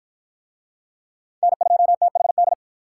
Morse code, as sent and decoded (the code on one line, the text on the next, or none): -. ..--- - .... -..
N2THD